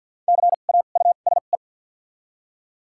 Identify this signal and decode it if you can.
Morse code, heard as a keyed tone, about 49 wpm, spaced as for 35 wpm, 700 Hz. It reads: CAUSE
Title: 2x Character Spacing - Top 300 Words - 50wpm